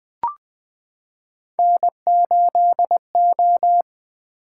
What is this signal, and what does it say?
Morse code, heard as a keyed tone, about 20 wpm, 700 Hz. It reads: N8O
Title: Three Letter-Number Combinations - 20wpm